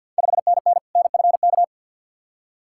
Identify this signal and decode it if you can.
Morse code, heard as a keyed tone, about 52 wpm, 700 Hz. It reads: 5NN NFK
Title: Ontario QSO Party Exchange 50wpm